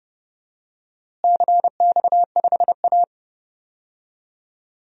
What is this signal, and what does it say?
Morse code, heard as a keyed tone, about 30 wpm, 700 Hz. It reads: CX5A
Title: DXCC-100 Call Signs - 4 Characters - Repeated 3x - 30wpm